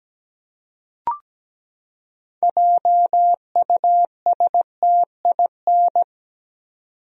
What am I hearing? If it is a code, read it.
Morse code, heard as a keyed tone, about 17 wpm, 700 Hz. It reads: JUSTIN
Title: USA Common Names - 17wpm